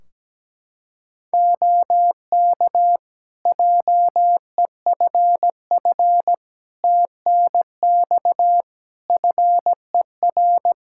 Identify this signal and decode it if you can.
Morse code, heard as a keyed tone, about 17 wpm, 700 Hz. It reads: OK JEFF TNX FER
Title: Practice QSO - Exchange 4 - Ending - 17wpm